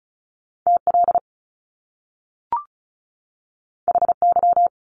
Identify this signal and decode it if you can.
Morse code, heard as a keyed tone, about 36 wpm, 700 Hz. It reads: TL HY
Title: Top Two Letter Combinations in Words 35wpm